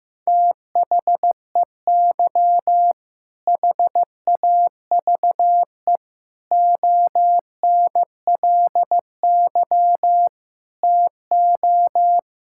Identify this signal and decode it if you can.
Morse code, heard as a keyed tone, about 15 wpm, 700 Hz. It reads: THEY HAVE ONLY TO
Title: Sentences from Top 1000 Words - Review Part 5 - 15wpm